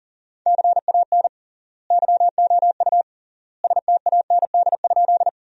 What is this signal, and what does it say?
Morse code, heard as a keyed tone, about 40 wpm, 700 Hz. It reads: CAN YOU STAND?
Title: 3x Word-Spacing - Sentences from Top 300 Words - 40wpm